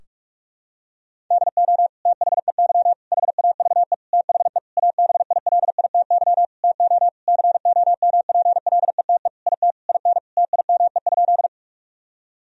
Morse code, heard as a keyed tone, about 41 wpm, 700 Hz. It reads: DO THEY HAVE THE ABILITY TO COMPLETE IT IN TIME?
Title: Sentences from Top 1000 Words - Review Part 3 - 45wpm